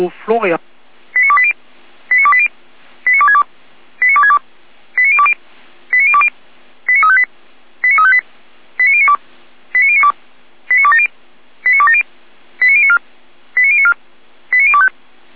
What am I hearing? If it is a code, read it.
Five-tone selective call (ZVEI), heard as quick runs of five tones, tones of about 70 ms, with speech. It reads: 89280 89280 89262 89262 89020 89020 89358 89358 89092 89092 89270 89270 89094 89094 89025